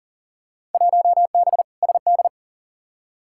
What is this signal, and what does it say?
Morse code, heard as a keyed tone, about 40 wpm, 700 Hz. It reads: 1B SD
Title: Field Day Exchanges Encore - 40wpm